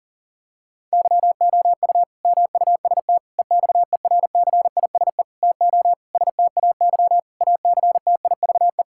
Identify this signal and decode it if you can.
Morse code, heard as a keyed tone, about 40 wpm, 700 Hz. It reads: YOU MUST EXERCISE TO STAY ACTIVE
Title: Sentences from Top 1000 Words - Review Part 6 - 40wpm